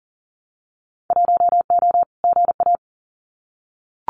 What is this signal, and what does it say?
Morse code, heard as a keyed tone, about 40 wpm, 700 Hz. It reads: 1O GA
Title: Winter Field Day Exchanges 40wpm